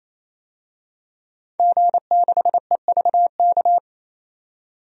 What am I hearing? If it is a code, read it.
Morse code, heard as a keyed tone, about 28 wpm, 700 Hz. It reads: G6EVK
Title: DXCC-100 Call Signs - 5 Characters - 28wpm